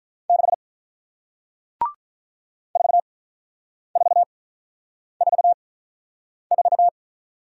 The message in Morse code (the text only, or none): 6 4444